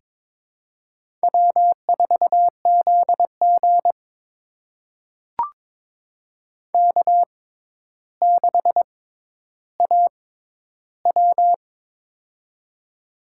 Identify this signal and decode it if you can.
Morse code, heard as a keyed tone, about 22 wpm, spaced as for 11 wpm, 700 Hz. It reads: W4ZG K6AW
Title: USA Call Signs - 4 Characters - 6x Character Spacing - 22wpm